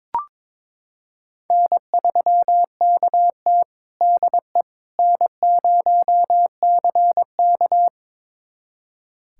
Morse code, heard as a keyed tone, about 22 wpm, 700 Hz. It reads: N3KT DE N0CK
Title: USA Callsign DE Callsign - 4 Characters - 22wpm